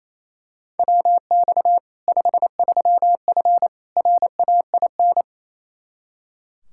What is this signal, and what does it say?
Morse code, heard as a keyed tone, about 28 wpm, 700 Hz. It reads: WX 53F RAIN
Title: Practice QSO - Exchange 2 - Info - 28wpm